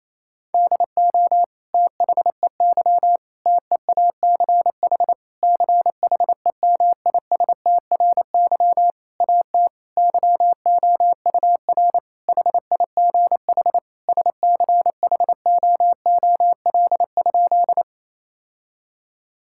Morse code, heard as a keyed tone, about 28 wpm, 700 Hz. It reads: DO THEY TEACH CHEMISTRY AT YOUR HIGH SCHOOL?
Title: Sentences from Top 1000 Words - Review Part 6 - 28wpm